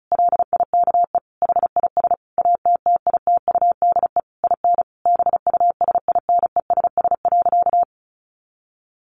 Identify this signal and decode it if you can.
Morse code, heard as a keyed tone, about 34 wpm, 700 Hz. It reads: LIKE HIS ATTITUDE IN BUSINESS.
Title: Sentences from Top 1300 Words 35wpm